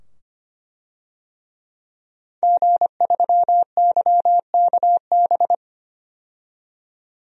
Morse code, heard as a keyed tone, about 25 wpm, 700 Hz. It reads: G3YKB
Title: DXCC-100 Call Signs - 5 Characters - 25wpm